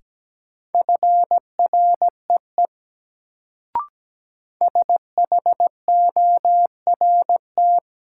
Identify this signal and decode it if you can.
Morse code, heard as a keyed tone, about 17 wpm, 700 Hz. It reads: FREE SHORT
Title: Sets of 5 Words Encore - New Word Drill - 17wpm